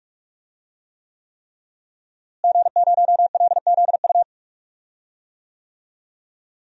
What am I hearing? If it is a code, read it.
Morse code, heard as a keyed tone, about 43 wpm, 700 Hz. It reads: G0LZU
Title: DXCC-100 Call Signs - 5 Characters - Repeated 3x - 45wpm